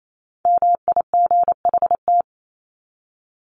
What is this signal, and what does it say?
Morse code, heard as a keyed tone, about 28 wpm, 700 Hz. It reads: MIGHT
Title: Top 300 Words 28wpm